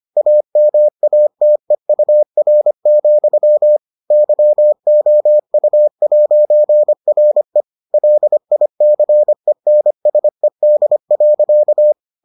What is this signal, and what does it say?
Morse code, heard as a keyed tone, about 25 wpm, 600 Hz. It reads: AMATEUR, YOU'RE LICENSED.